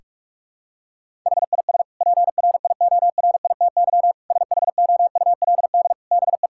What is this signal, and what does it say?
Morse code, heard as a keyed tone, about 46 wpm, 700 Hz. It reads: HIS PRIORITY SHOULD BE
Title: Sentences from Top 1600 Words 45wpm